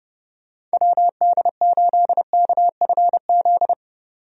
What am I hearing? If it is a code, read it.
Morse code, heard as a keyed tone, about 30 wpm, 700 Hz. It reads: WD8KFZ
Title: USA Call Signs - 6 Characters - 30wpm